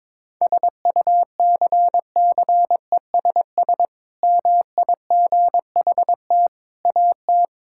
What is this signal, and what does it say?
Morse code, heard as a keyed tone, about 22 wpm, 700 Hz. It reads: SUCCESS MIGHT AT
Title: Sentences from Top 1000 Words - Review Part 6 - 22wpm